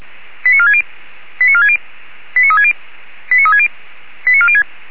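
Five-tone selective call (ZVEI), heard as quick runs of five tones, tones of about 70 ms. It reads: 89460 89460 89360 89360 89406
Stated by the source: Five-tone ZVEI selective call received on an SDR.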